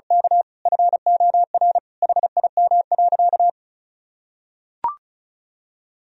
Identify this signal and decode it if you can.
Morse code, heard as a keyed tone, about 36 wpm, 700 Hz. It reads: K FOR HIM.
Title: Sentences from Top 1000 Words - Review Part 4 - 35wpm